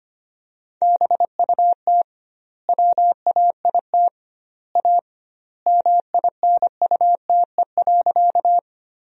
Morse code, heard as a keyed tone, about 25 wpm, 700 Hz. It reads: BUT WAIT A MINUTE.
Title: 2x Word-Spacing - Sentences from Top 500 Words - 25wpm